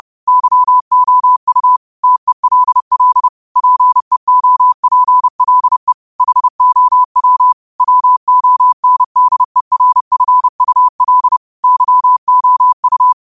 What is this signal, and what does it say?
Morse code, heard as a keyed tone, about 30 wpm, 1000 Hz. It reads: YOU TELL PEOPLE HOW WONDERFUL YOU
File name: qotdfiles / 30wpm